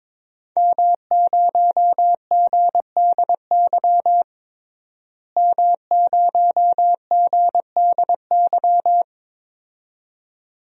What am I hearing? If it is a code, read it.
Morse code, heard as a keyed tone, about 22 wpm, 700 Hz. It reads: M0GDY M0GDY
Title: DXCC-100 Call Signs - 5 Characters - Repeated 2x - 22wpm